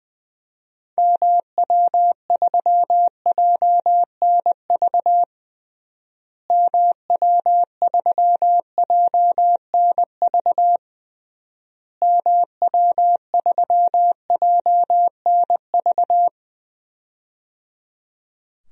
Morse code, heard as a keyed tone, about 21 wpm, 700 Hz. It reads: MW3JNV MW3JNV MW3JNV
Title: DXCC-100 Call Signs - 6 Characters - Repeated 3x - 20wpm